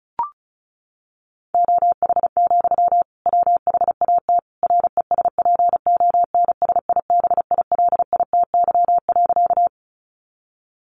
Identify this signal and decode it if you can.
Morse code, heard as a keyed tone, about 34 wpm, 700 Hz. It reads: OH, WHAT RESPONSIBILITY.
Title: Sentences from Top 1200 Words 35wpm